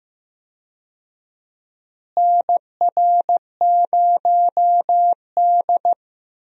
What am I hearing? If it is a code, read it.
Morse code, heard as a keyed tone, about 15 wpm, 700 Hz. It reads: NR0D
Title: USA Call Signs - 4 Characters - Repeated 3x - 15wpm